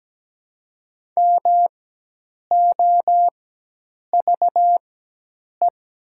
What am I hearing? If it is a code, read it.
Morse code, heard as a keyed tone, about 17 wpm, spaced as for 8 wpm, 700 Hz. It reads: MOVE
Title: 4x Character Spacing - Top 500 Words - 17wpm